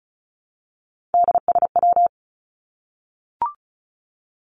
Morse code, heard as a keyed tone, about 34 wpm, 700 Hz. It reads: DSW
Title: Abbreviations Repeated 3x - 35wpm